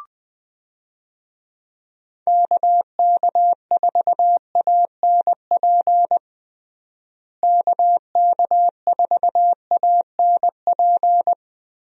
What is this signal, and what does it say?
Morse code, heard as a keyed tone, about 20 wpm, 700 Hz. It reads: KK4ANP KK4ANP
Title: USA Call Signs - 6 Characters - Repeated 2x - 20wpm